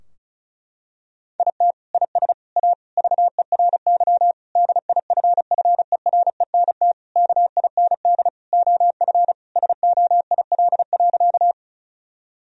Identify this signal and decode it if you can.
Morse code, heard as a keyed tone, about 36 wpm, 700 Hz. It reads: IT IS A VERY DIFFERENT KIND OF SOIL.
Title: Sentences from Top 1000 Words - Review Part 4 - 35wpm